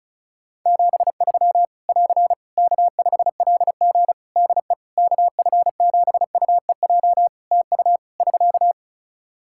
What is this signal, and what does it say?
Morse code, heard as a keyed tone, about 36 wpm, 700 Hz. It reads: 73 <AR> K5LG DE KF7UEJ TU <SK>
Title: K1USN SST QSO Practice 35wpm